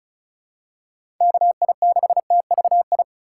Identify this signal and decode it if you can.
Morse code, heard as a keyed tone, about 34 wpm, 700 Hz. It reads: KI6TVI
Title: USA Call Signs - 6 Characters - 4x Character Spacing - 35wpm